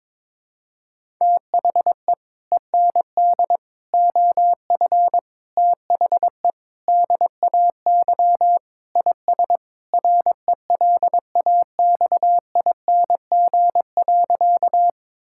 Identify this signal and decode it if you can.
Morse code, heard as a keyed tone, about 22 wpm, 700 Hz. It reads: THE END OF THE DAY IS RELAXING.
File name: practice / 22wpm / Sentences with 3-Word Phrases - 22wpm